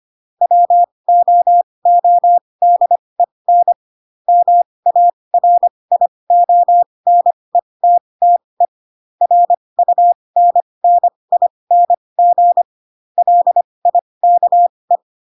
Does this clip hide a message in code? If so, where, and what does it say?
Yes: Morse code from 0.4 to 15.0 s, heard as a keyed tone, about 25 wpm, spaced as for 20 wpm, 700 Hz. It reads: WOODEN MARIONETTE RUNNING LIKE